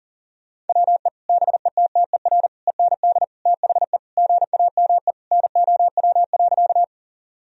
Morse code, heard as a keyed tone, about 40 wpm, 700 Hz. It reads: WE BETTER END THE GAME NOW.
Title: Sentences from Top 1000 Words - Review Part 3 - 40wpm